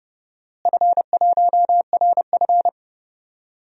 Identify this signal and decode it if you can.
Morse code, heard as a keyed tone, about 30 wpm, 700 Hz. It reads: F1RF
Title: DXCC-100 Call Signs - 4 Characters - Repeated 3x - 30wpm